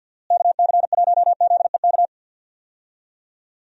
Morse code, heard as a keyed tone, about 50 wpm, 700 Hz. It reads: KC1ZEK
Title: USA Call Signs - 6 Characters - Repeated 3x - 50wpm